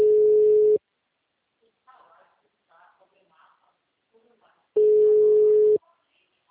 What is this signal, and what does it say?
Ringback tone (Europe, 425 Hz), with speech.
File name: out- -2048- - - .67941